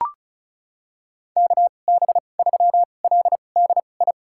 Morse code, heard as a keyed tone, about 34 wpm, spaced as for 25 wpm, 700 Hz. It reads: KB3LDI
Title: USA Call Signs - 2x Character Spacing - 35wpm